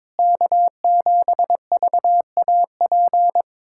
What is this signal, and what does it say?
Morse code, heard as a keyed tone, about 21 wpm, 700 Hz. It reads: K7VAP
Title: K1USN SST QSO Practice - 22wpm